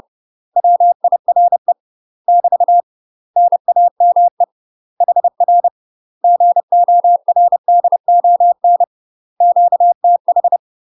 Morse code, heard as a keyed tone, about 30 wpm, 700 Hz. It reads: WIRE <BT> NAME HR GORDON QTH
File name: UK MASTER QSOs TRAINING / 30WPM / UK MASTER QSOs 0005